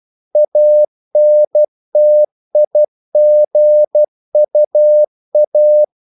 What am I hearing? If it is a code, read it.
Morse code, heard as a keyed tone, about 12 wpm, 600 Hz. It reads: ANTIGUA
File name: CW / Practice / NationalCapitals12